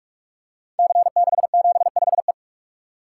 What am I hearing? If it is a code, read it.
Morse code, heard as a keyed tone, about 44 wpm, 700 Hz. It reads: C675E
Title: Code Groups - 5 Letter:Numbers - 45wpm